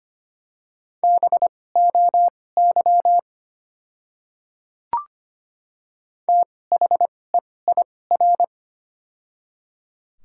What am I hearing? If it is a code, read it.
Morse code, heard as a keyed tone, about 25 wpm, spaced as for 18 wpm, 700 Hz. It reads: BOY THEIR
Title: 2x Character Spacing - Top 200 Words - 25wpm